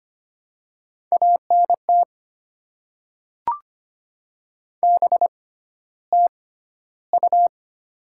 Morse code, heard as a keyed tone, about 25 wpm, spaced as for 13 wpm, 700 Hz. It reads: ANT BTU
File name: Top 20 QSO Elements - 6x Character Spacing - 25wpm